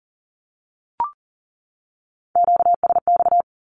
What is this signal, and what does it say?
Morse code, heard as a keyed tone, about 40 wpm, 700 Hz. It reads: QSX
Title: Q-Codes 40wpm